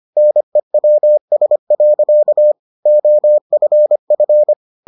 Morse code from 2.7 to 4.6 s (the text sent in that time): OFF